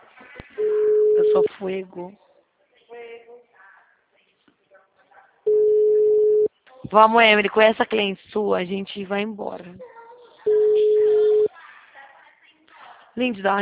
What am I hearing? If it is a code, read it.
Ringback tone (Europe, 425 Hz), with speech.
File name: out- -2050- - - .35648